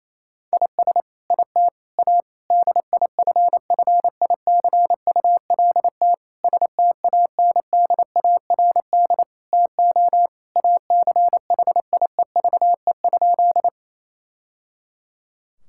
Morse code, heard as a keyed tone, about 28 wpm, 700 Hz. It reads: IS IT A DIFFICULT STANDARD TO ACHIEVE?
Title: Sentences from Top 1000 Words - Review Part 6 - 28wpm